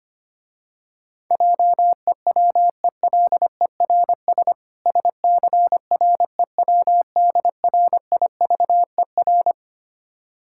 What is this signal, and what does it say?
Morse code, heard as a keyed tone, about 25 wpm, 690 Hz. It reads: JEWELERS SCREWDRIVER
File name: practice / 25wpm / Difficult Words - Speed-Racing with Hints - 25wpm